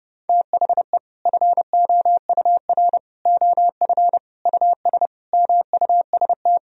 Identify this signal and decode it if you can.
Morse code, heard as a keyed tone, about 30 wpm, 700 Hz. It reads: THE FOUR OF US MUST
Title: Sentences from Top 1000 Words - Review Part 1 - 30wpm